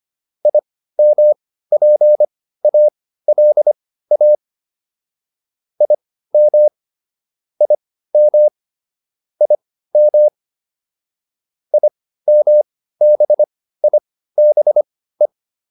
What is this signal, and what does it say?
Morse code, heard as a keyed tone, about 25 wpm, spaced as for 15 wpm, 600 Hz. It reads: IMPALA IM IM IM IMBIBE